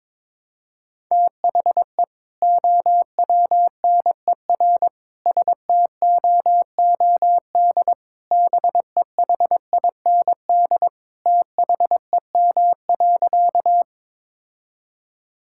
Morse code, heard as a keyed tone, about 22 wpm, 700 Hz. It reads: THE OWNER STOOD BEHIND THEM.
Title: Sentences from Top 1600 Words 22wpm